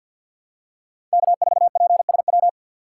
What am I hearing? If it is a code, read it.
Morse code, heard as a keyed tone, about 48 wpm, 700 Hz. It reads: K4PSW
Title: USA Call Signs - 4x Character Spacing - 50wpm